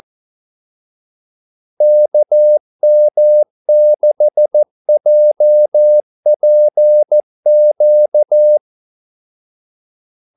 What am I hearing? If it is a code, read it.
Morse code, heard as a keyed tone, about 14 wpm, 600 Hz. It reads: KM6JPQ
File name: Callsigns-14wpm